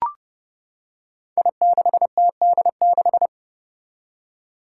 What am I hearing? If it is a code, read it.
Morse code, heard as a keyed tone, about 30 wpm, 700 Hz. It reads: I6TD6
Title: Five Challenging Letter-Number Combinations 30wpm